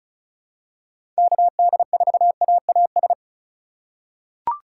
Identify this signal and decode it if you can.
Morse code, heard as a keyed tone, about 34 wpm, 700 Hz. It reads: KD4AAS
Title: USA Call Signs - 6 Characters - 2x Character Spacing - 35wpm